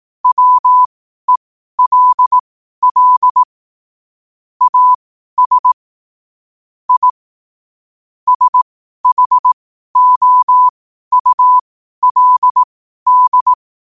Morse code, heard as a keyed tone, about 18 wpm, spaced as for 12 wpm, 1000 Hz. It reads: WELL AS I SHOULD
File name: qotdfiles / 12wpm